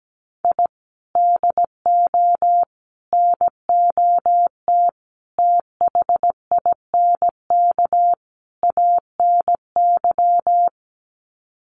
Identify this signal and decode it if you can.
Morse code, heard as a keyed tone, about 17 wpm, 700 Hz. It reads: I DO NOT THINK ANY
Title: Sets of 5 Words Encore 17wpm